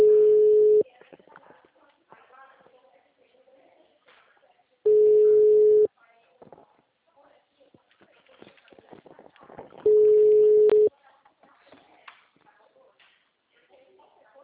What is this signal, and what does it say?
Ringback tone (Europe, 425 Hz), with speech.